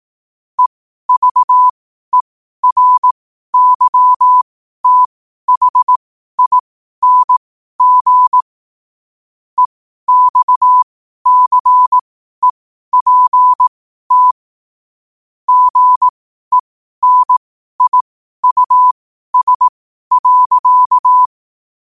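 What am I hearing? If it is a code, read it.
Morse code, heard as a keyed tone, about 18 wpm, spaced as for 12 wpm, 1000 Hz. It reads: EVERYTHING EXCEPT GENIUS.
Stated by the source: Morse code quotes at 12 words per minute, by ai4qr.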